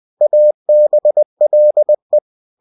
Morse code, heard as a keyed tone, about 20 wpm, 600 Hz. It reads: ABLE
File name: cw-books / winnie-the-pooh / winnie-20wpm-page-0026